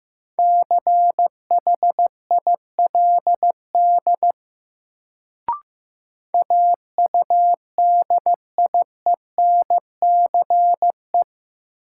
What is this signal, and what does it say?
Morse code, heard as a keyed tone, about 15 wpm, 700 Hz. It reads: CHILD AUDIENCE
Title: Sentences from Top 800 Words - New Word Drill - 15wpm